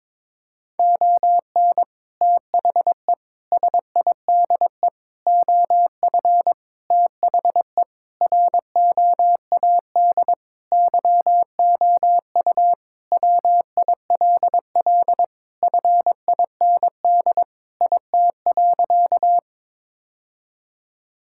Morse code, heard as a keyed tone, about 22 wpm, 700 Hz. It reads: ON THE SIDE OF THE ROAD YOU WILL FIND IT.
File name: practice / 22wpm / Sentences with 3-Word Phrases - 22wpm